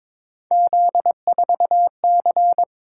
Morse code, heard as a keyed tone, about 22 wpm, 700 Hz. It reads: Z4C
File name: practice / 22wpm / Three Letter-Number Combinations - 22wpm